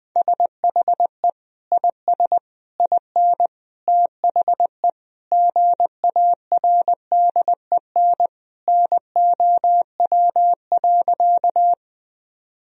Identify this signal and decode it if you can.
Morse code, heard as a keyed tone, about 20 wpm, 700 Hz. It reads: SHE IS IN THE GARDEN NOW.
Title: Sentences from Top 1000 Words - Review Part 4 - 20wpm